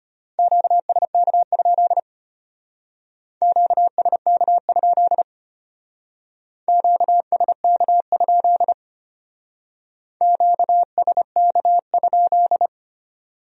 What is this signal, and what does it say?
Morse code, heard as a keyed tone, about 32 wpm, 700 Hz. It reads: QSK? QSK? QSK? QSK?